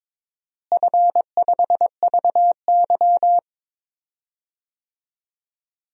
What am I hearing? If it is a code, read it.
Morse code, heard as a keyed tone, about 22 wpm, 700 Hz. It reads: F5VY
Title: DXCC-100 Call Signs - 4 Characters - Repeated 3x - 22wpm